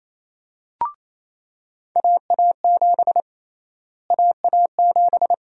Morse code, heard as a keyed tone, about 28 wpm, 700 Hz. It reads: AA7 AA7
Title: USA Call Sign Prefixes - Repeated 2x - 28wpm